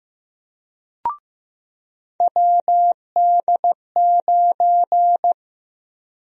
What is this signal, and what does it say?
Morse code, heard as a keyed tone, about 15 wpm, 700 Hz. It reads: WD9
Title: USA Call Sign Prefixes - 15wpm